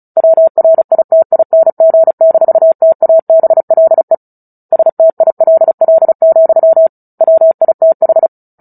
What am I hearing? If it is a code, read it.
Morse code, heard as a keyed tone, about 34 wpm, 650 Hz. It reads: WRITING-TABLE STILL, WITH
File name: Book Two / 09-Chapter 9 - 35 WPM